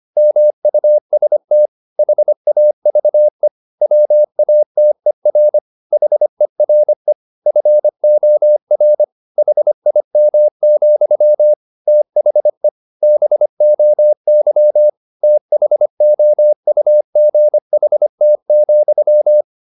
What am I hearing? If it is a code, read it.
Morse code, heard as a keyed tone, about 25 wpm, 600 Hz. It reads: MUST HAVE WATER HERE FOR HIM, THE BOY THOUGHT,
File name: the old man and the sea part 1 25wpm